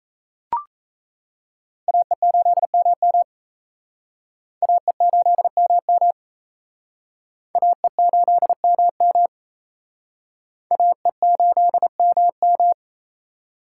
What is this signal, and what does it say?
Morse code, heard as a keyed tone, about 34 wpm, 700 Hz. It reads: AE8MM AE8MM AE8MM AE8MM
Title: Speed Racing - USA 2x2 Call Signs 28wpm